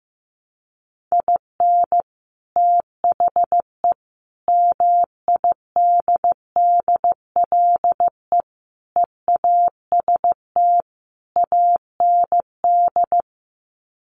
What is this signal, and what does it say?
Morse code, heard as a keyed tone, about 15 wpm, 700 Hz. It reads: IN THE MIDDLE EAST AND
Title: Sets of 5 Words Encore 15wpm